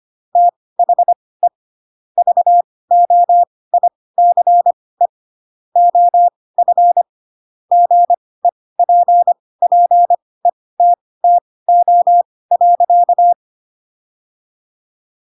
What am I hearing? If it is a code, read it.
Morse code, heard as a keyed tone, about 25 wpm, spaced as for 18 wpm, 700 Hz. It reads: THE VOICE OF GEPPETTO.
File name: Pinocchio / Pinocchio ch6 12 25 ik1hge